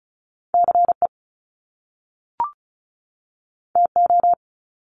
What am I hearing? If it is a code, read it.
Morse code, heard as a keyed tone, about 36 wpm, 700 Hz. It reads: CE TO
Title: Top Two Letter Combinations in Words 35wpm